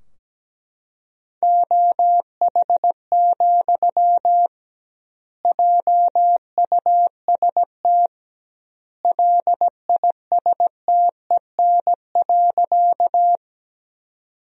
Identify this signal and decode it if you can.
Morse code, heard as a keyed tone, about 17 wpm, 700 Hz. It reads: OH, JUST LISTEN.
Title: 2x Word-Spacing - Sentences from Top 500 Words - 17wpm